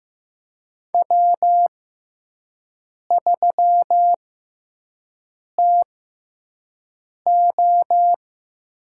Morse code, heard as a keyed tone, about 16 wpm, spaced as for 5 wpm, 700 Hz. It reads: W3TO
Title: USA Call Signs - 5 Characters - 6x Character Spacing - 15wpm